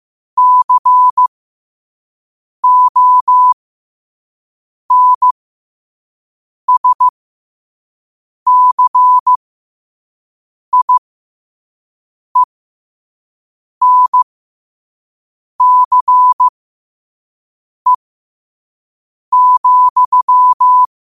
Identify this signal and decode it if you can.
Morse code, heard as a keyed tone, about 15 wpm, spaced as for 5 wpm, 1000 Hz. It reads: CONSCIENCE,
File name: qotdfiles / 5wpm